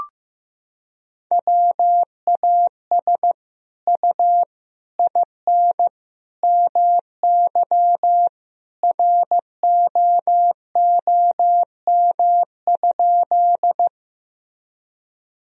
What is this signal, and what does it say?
Morse code, heard as a keyed tone, about 16 wpm, 700 Hz. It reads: WAS U IN MY ROOM?
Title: Sentences from Top 300 Words and Abbreviations 15wpm